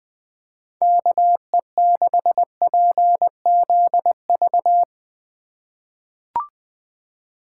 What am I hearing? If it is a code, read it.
Morse code, heard as a keyed tone, about 20 wpm, 700 Hz. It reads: KE6PZV
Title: USA Call Signs - 6 Characters - 8x Character Spacing - 20wpm